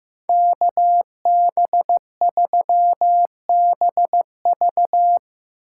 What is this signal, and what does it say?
Morse code, heard as a keyed tone, about 15 wpm, 700 Hz. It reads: KB3BV
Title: USA Call Signs - 6x Character Spacing - 15wpm